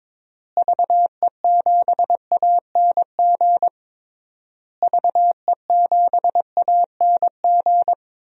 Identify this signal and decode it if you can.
Morse code, heard as a keyed tone, about 21 wpm, 700 Hz. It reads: VE7ANG VE7ANG
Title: DXCC-100 Call Signs - 6 Characters - Repeated 3x - 22wpm